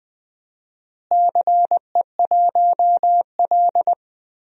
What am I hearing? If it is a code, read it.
Morse code, heard as a keyed tone, about 21 wpm, 700 Hz. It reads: CE1L
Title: DXCC-100 Call Signs - 4 Characters - Repeated 3x - 20wpm